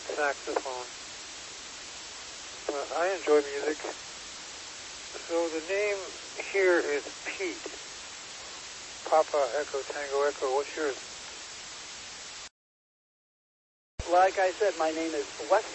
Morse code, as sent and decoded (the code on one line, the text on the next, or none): none